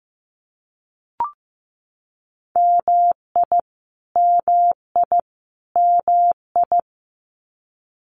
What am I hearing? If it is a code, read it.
Morse code, heard as a keyed tone, about 15 wpm, 700 Hz. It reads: MI MI MI
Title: US States Repeated 3x 15wpm